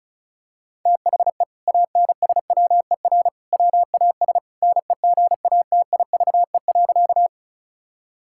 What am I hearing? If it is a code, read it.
Morse code, heard as a keyed tone, about 36 wpm, 700 Hz. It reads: THE ANSWER WAS NEGATIVE.
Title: Sentences from Top 1000 Words - Review Part 6 - 35wpm